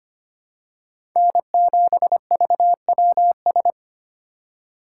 Morse code, heard as a keyed tone, about 25 wpm, 700 Hz. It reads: N7VWS
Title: USA Call Signs - 5 Characters - 6x Character Spacing - 25wpm